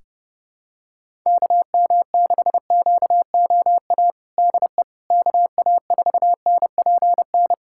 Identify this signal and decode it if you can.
Morse code, heard as a keyed tone, about 30 wpm, 700 Hz. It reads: KM6QOA DE KA4NPN
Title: USA Callsign DE Callsign - 6 Characters - 30wpm